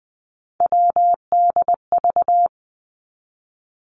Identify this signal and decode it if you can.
Morse code, heard as a keyed tone, about 19 wpm, 700 Hz. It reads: WDV
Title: Three Challenging Letter-Number Combinations 20wpm